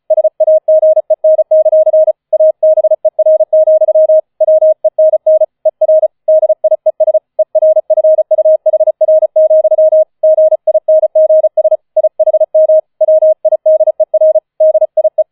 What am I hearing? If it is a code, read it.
Morse code, heard as a keyed tone, about 34 wpm, 630 Hz. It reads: SAGEN; ABER, WENN ER DIES ERFUHR, GINGS IHM WIDER DIE